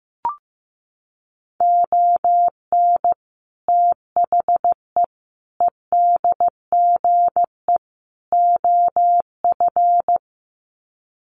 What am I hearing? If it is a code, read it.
Morse code, heard as a keyed tone, about 15 wpm, 700 Hz. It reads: ON THE EDGE OF
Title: Sets of 4 Words 15wpm